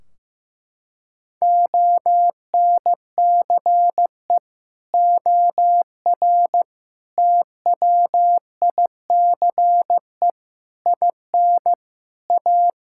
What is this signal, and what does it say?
Morse code, heard as a keyed tone, about 15 wpm, 700 Hz. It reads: ONCE OR TWICE IN A
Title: Sentences from Top 1600 Words 15wpm